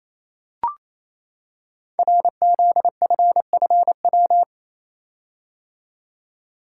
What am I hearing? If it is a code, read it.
Morse code, heard as a keyed tone, about 28 wpm, 700 Hz. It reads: RZFFW
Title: Code Groups - 5 Letters - 28wpm